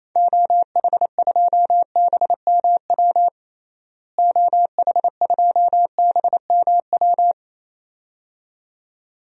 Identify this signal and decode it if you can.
Morse code, heard as a keyed tone, about 27 wpm, 700 Hz. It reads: OH2BMW OH2BMW
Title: DXCC-100 Call Signs - 6 Characters - Repeated 2x - 28wpm